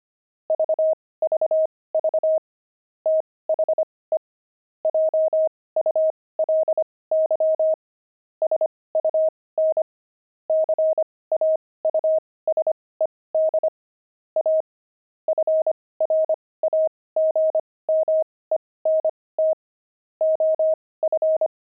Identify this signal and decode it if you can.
Morse code, heard as a keyed tone, about 25 wpm, spaced as for 18 wpm, 630 Hz. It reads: VVV THE JULY SUN CAUSED A FRAGMENT OF
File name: pangram-fs-18-cs-25-fragment